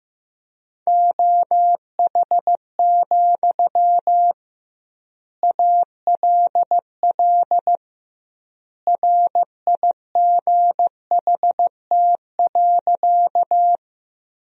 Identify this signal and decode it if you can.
Morse code, heard as a keyed tone, about 16 wpm, 700 Hz. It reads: OH, ALL RIGHT.
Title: 2x Word-Spacing - Sentences from Top 500 Words - 15wpm